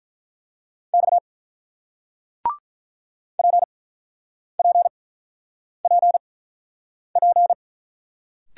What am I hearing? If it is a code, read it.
Morse code, heard as a keyed tone, about 45 wpm, spaced as for 9 wpm, 700 Hz. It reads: X PPPP